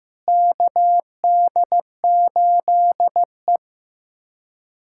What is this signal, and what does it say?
Morse code, heard as a keyed tone, about 15 wpm, 700 Hz. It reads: KD8E
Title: USA Callsign DE Callsign - 4 Characters - 15wpm